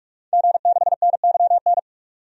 Morse code, heard as a keyed tone, about 46 wpm, 700 Hz. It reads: G6NYN
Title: DXCC-100 Call Signs - 5 Characters - 45wpm